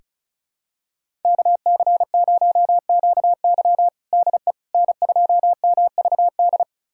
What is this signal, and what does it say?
Morse code, heard as a keyed tone, about 34 wpm, 700 Hz. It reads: KC0QY DE N2MVD
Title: USA Callsign DE Callsign - 5 Characters - 35wpm